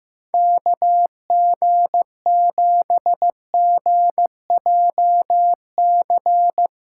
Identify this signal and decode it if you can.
Morse code, heard as a keyed tone, about 15 wpm, 700 Hz. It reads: KG7GJC
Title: USA Callsign DE Callsign 15wpm